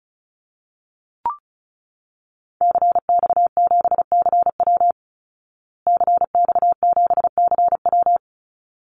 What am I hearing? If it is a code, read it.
Morse code, heard as a keyed tone, about 34 wpm, 700 Hz. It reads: CX7CW CX7CW
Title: Call Signs Repeated 3x - 35wpm